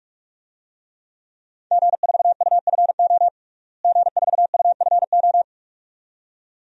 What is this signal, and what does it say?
Morse code, heard as a keyed tone, about 46 wpm, 700 Hz. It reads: G4UFO G4UFO
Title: DXCC-100 Call Signs - 5 Characters - Repeated 2x - 45wpm